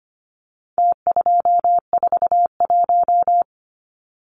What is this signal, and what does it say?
Morse code, heard as a keyed tone, about 25 wpm, 700 Hz. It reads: T241
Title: Numbers - Contest Style - 4-Digit 25wpm